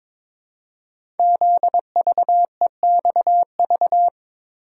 Morse code, heard as a keyed tone, about 22 wpm, 700 Hz. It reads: ZVEXV
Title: Code Groups - 5 Letters - 22wpm